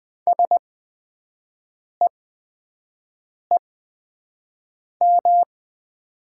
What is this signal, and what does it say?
Morse code, heard as a keyed tone, about 20 wpm, spaced as for 5 wpm, 700 Hz. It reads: SEEM
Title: 8x Character Spacing - Top 500 Words - 20wpm